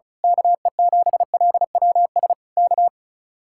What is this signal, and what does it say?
Morse code, heard as a keyed tone, about 32 wpm, 700 Hz. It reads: KE7LWS K
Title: Speed-Racing Common QSO Phrases - 35wpm